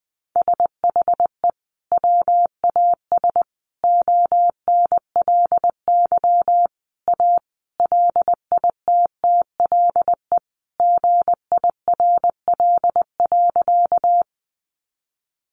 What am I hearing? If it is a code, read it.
Morse code, heard as a keyed tone, about 20 wpm, 700 Hz. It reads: SHE WAS ONLY A LITTLE GIRL.
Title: Sentences from Top 400 Words 20wpm